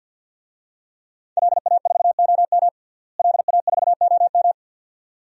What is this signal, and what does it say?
Morse code, heard as a keyed tone, about 50 wpm, 700 Hz. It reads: LA4OM LA4OM
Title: DXCC-100 Call Signs - 5 Characters - Repeated 2x - 50wpm